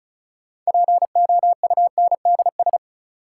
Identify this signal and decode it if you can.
Morse code, heard as a keyed tone, about 36 wpm, 700 Hz. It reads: POUNDS
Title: Sentences from Top 1000 Words - Review Part 2 - New Word Drill - 35wpm